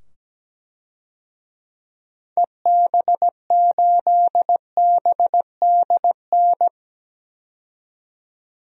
Morse code, heard as a keyed tone, about 17 wpm, 700 Hz. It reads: EB8BDN
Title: DXCC-100 Call Signs - 6 Characters - 17wpm